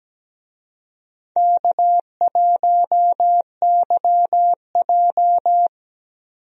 Morse code, heard as a keyed tone, about 17 wpm, 700 Hz. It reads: K1YJ
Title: USA Call Signs - 4 Characters - 17wpm